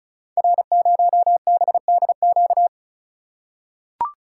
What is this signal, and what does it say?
Morse code, heard as a keyed tone, about 36 wpm, 690 Hz. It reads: R0BDQ
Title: Code Groups - 5 Letter:Numbers - 35wpm